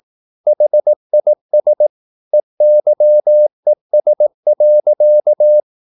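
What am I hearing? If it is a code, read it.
Morse code, heard as a keyed tone, about 18 wpm, 600 Hz. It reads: HIS EYES.
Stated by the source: Morse practice at 18 wpm, English text, from CW med Gnister.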